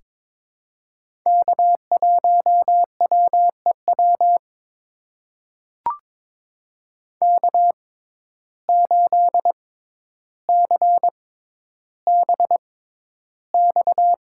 Morse code, heard as a keyed tone, about 22 wpm, spaced as for 11 wpm, 700 Hz. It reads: K1WEW K8CBX
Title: USA Call Signs - 5 Characters - 6x Character Spacing - 22wpm